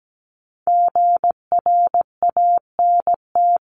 Morse code, heard as a keyed tone, about 17 wpm, 700 Hz. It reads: GRANT
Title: CWT Exchange 17wpm